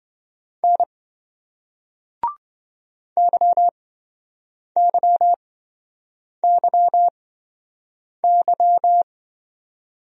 Morse code, heard as a keyed tone, about 26 wpm, spaced as for 7 wpm, 700 Hz. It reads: N YYYY